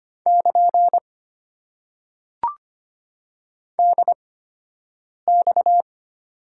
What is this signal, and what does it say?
Morse code, heard as a keyed tone, about 25 wpm, spaced as for 7 wpm, 690 Hz. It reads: <KN> DX
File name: Top 90 QSO Elements - 8x Character Spacing - 25wpm